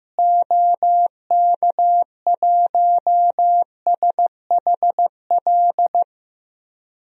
Morse code, heard as a keyed tone, about 15 wpm, 700 Hz. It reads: OK1SHL
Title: DXCC-100 Call Signs - 6 Characters - Repeated 3x - 15wpm